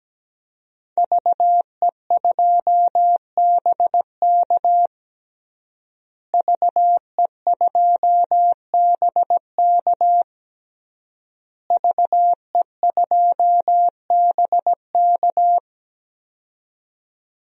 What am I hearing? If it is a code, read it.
Morse code, heard as a keyed tone, about 17 wpm, 700 Hz. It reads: VE2BK VE2BK VE2BK
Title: DXCC-100 Call Signs - 5 Characters - Repeated 3x - 17wpm